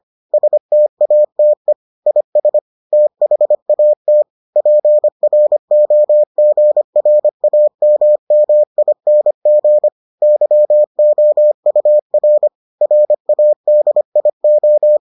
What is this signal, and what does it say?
Morse code, heard as a keyed tone, about 25 wpm, 600 Hz. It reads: STATE IS THAT PROGRAMMING YOUR RADIO